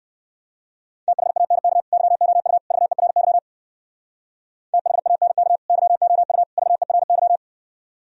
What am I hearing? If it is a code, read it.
Morse code, heard as a keyed tone, about 61 wpm, 700 Hz. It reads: THANK YOU VERY THANK YOU VERY